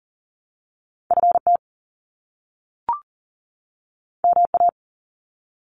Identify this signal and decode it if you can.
Morse code, heard as a keyed tone, about 40 wpm, 700 Hz. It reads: FT MA
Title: Top Two Letter Combinations in Words 40wpm